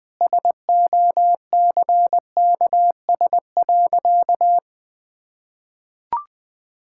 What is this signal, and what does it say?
Morse code, heard as a keyed tone, about 20 wpm, 700 Hz. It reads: SOCKS.
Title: Easy Sentences with Hints - 20wpm